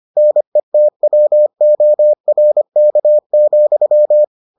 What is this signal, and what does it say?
Morse code, heard as a keyed tone, about 25 wpm, 600 Hz. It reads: NETWORK,